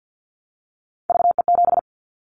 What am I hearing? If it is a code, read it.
Morse code, heard as a keyed tone, about 49 wpm, 700 Hz. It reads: VE7
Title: Callsign Prefixes for Top 100 DXCC - 50wpm